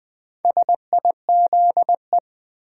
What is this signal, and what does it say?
Morse code, heard as a keyed tone, about 20 wpm, 700 Hz. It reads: SIZE
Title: Build A Word - 1 Letter At A Time - 3 to 5-Letter Words - 20wpm